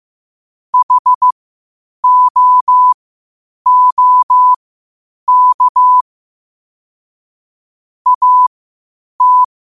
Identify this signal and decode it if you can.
Morse code, heard as a keyed tone, about 15 wpm, spaced as for 8 wpm, 1000 Hz. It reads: HOOK AT